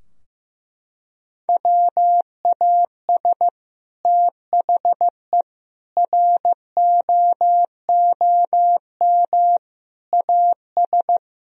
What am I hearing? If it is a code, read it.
Morse code, heard as a keyed tone, about 15 wpm, 700 Hz. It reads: WAS THE ROOM AS
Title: Sentences from Top 1000 Words - Review Part 4 - 15wpm